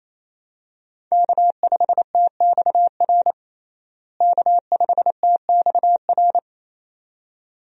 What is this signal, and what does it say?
Morse code, heard as a keyed tone, about 28 wpm, 700 Hz. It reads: K5TXR K5TXR
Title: USA Call Signs - 5 Characters - Repeated 2x - 28wpm